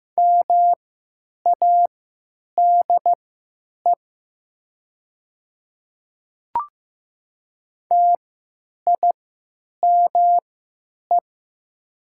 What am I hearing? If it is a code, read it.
Morse code, heard as a keyed tone, about 15 wpm, spaced as for 9 wpm, 700 Hz. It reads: MADE TIME
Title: 3x Character Spacing - Top 200 Words - 15wpm